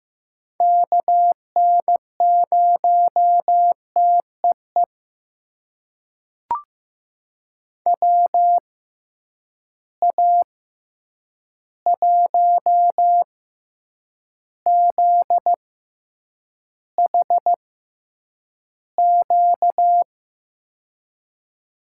Morse code, heard as a keyed tone, about 15 wpm, spaced as for 8 wpm, 700 Hz. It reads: KN0TEE WA1ZHQ
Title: USA Call Signs - 6 Characters - 6x Character Spacing - 15wpm